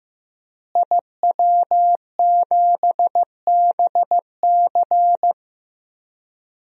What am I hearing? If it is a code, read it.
Morse code, heard as a keyed tone, about 15 wpm, 700 Hz. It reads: IW7BC
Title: DXCC-100 Call Signs - 5 Characters - Repeated 3x - 15wpm